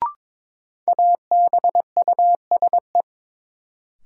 Morse code, heard as a keyed tone, about 22 wpm, 700 Hz. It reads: ABUSE
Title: Sentences from Top 1300 Words - New Word Drill - 22wpm